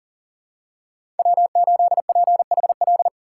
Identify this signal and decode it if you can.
Morse code, heard as a keyed tone, about 40 wpm, 700 Hz. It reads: W8PHL
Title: USA Call Signs - 5 Characters - 4x Character Spacing - 40wpm